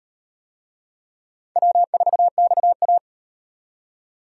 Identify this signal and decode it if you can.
Morse code, heard as a keyed tone, about 38 wpm, 700 Hz. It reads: W4XA